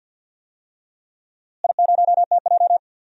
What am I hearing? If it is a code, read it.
Morse code, heard as a keyed tone, about 46 wpm, 700 Hz. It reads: I0TJ
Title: DXCC-100 Call Signs - 4 Characters - Repeated 3x - 50wpm